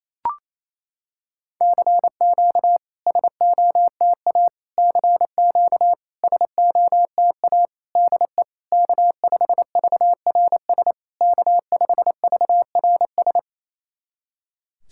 Morse code, heard as a keyed tone, about 27 wpm, 690 Hz. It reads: CQ SOTA CQ SOTA DE K5VRS K5VRS
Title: SOTA QSO Practice 28wpm